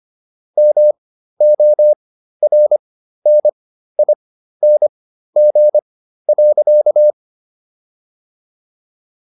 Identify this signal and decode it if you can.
Morse code, heard as a keyed tone, about 25 wpm, spaced as for 13 wpm, 600 Hz. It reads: MORNING.